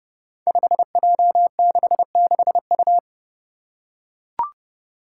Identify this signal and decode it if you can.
Morse code, heard as a keyed tone, about 28 wpm, 700 Hz. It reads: 5J66U
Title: Five Challenging Letter-Number Combinations 30wpm